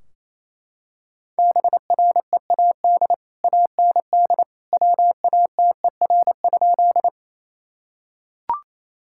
Morse code, heard as a keyed tone, about 28 wpm, 700 Hz. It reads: BREAD AND WATER?
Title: Sentences from Top 1600 Words 28wpm